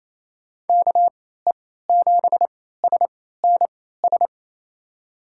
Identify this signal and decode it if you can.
Morse code, heard as a keyed tone, about 28 wpm, spaced as for 16 wpm, 700 Hz. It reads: KE7SNS
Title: USA Call Signs - 6 Characters - 3x Character Spacing - 28wpm